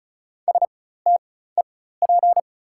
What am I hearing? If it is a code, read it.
Morse code, heard as a keyed tone, about 36 wpm, spaced as for 16 wpm, 690 Hz. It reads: STEP
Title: 4x Character Spacing - Top 500 Words - 35wpm